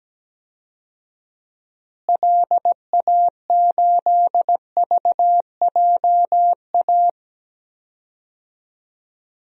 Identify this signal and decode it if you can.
Morse code, heard as a keyed tone, about 17 wpm, 700 Hz. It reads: LA8VJA
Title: DXCC-100 Call Signs - 6 Characters - Repeated 3x - 17wpm